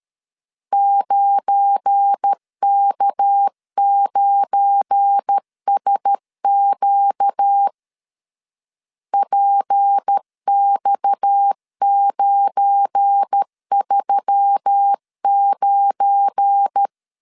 Morse code, heard as a keyed tone, about 13 wpm, 800 Hz. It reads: 9K9SQ PX939
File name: LEZIONI / 7 GF9Z / 7 GF9 12wpm 2319